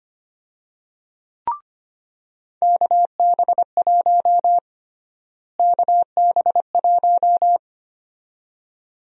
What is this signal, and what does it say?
Morse code, heard as a keyed tone, about 25 wpm, 700 Hz. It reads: KB1 KB1
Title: USA Call Sign Prefixes - Repeated 2x - 25wpm